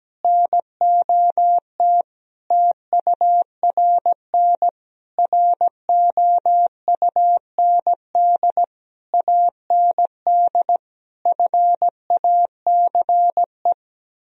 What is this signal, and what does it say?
Morse code, heard as a keyed tone, about 17 wpm, 700 Hz. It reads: NOT TURN ROUND AND FACE
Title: Sentences from Top 1000 Words - Review Part 2 - 17wpm